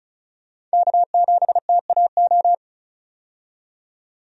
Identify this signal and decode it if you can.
Morse code, heard as a keyed tone, about 36 wpm, 700 Hz. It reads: K7TAO
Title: USA Call Signs - 5 Characters - 35wpm